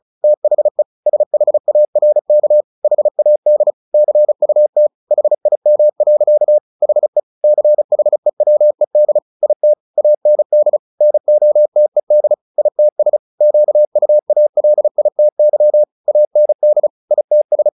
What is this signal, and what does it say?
Morse code, heard as a keyed tone, about 34 wpm, 600 Hz. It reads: THE SHARK HAD CUT HIM. HE CHEWED IT AND NOTED ITS QUALITY AND ITS
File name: the old man and the sea 2 35wpm